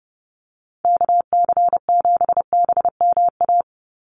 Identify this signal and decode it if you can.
Morse code, heard as a keyed tone, about 30 wpm, 700 Hz. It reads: KC7BMA
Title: USA Call Signs 30wpm